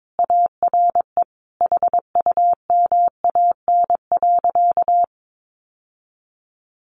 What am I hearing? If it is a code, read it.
Morse code, heard as a keyed tone, about 22 wpm, 700 Hz. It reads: ARE HUMAN.
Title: Sentences from Top 600 Words 22wpm